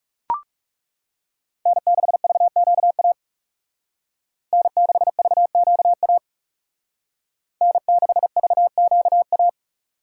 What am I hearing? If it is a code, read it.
Morse code, heard as a keyed tone, about 36 wpm, 700 Hz. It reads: N6VQA N6VQA N6VQA
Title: Speed Racing - USA 1x3 Call Signs 30wpm